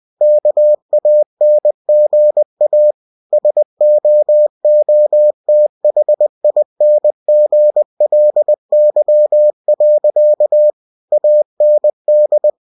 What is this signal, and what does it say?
Morse code, heard as a keyed tone, about 20 wpm, 600 Hz. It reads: KANGA SOOTHINGLY. AND
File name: cw-books / winnie-the-pooh / winnie-20wpm-page-0072